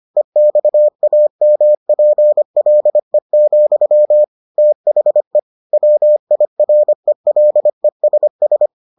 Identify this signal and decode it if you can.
Morse code, heard as a keyed tone, about 25 wpm, 600 Hz. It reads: EXAMPLE, THE WIRELESS